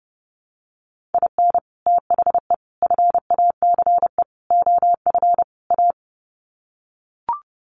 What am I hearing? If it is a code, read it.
Morse code, heard as a keyed tone, about 30 wpm, 700 Hz. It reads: IN THE FACE OF A
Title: Sets of 5 Words Encore 30wpm